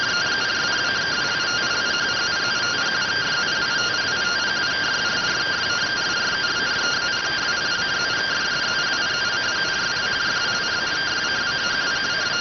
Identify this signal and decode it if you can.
Two-tone teleprinter signal, about 170 Hz shift.